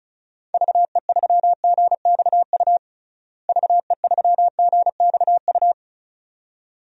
Morse code, heard as a keyed tone, about 36 wpm, 700 Hz. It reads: VE3GXU VE3GXU
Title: DXCC-100 Call Signs - 6 Characters - Repeated 2x - 35wpm